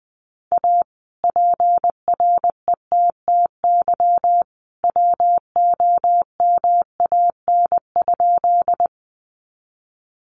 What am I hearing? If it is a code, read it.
Morse code, heard as a keyed tone, about 20 wpm, 700 Hz. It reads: A PRETTY WOMAN?
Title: Sentences from Top 1100 Words 20wpm